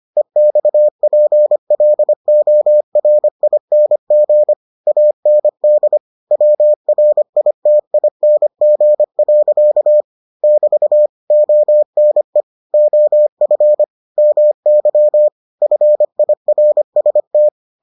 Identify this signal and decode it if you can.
Morse code, heard as a keyed tone, about 25 wpm, 600 Hz. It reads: EXPLORING AND WRITING. <BT> ONE OF MY FIRST